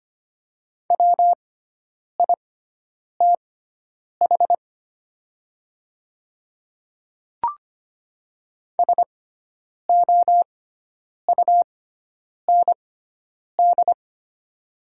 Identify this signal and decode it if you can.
Morse code, heard as a keyed tone, about 25 wpm, spaced as for 9 wpm, 700 Hz. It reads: WITH SOUND
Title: 6x Character Spacing - Top 100 Words - 25wpm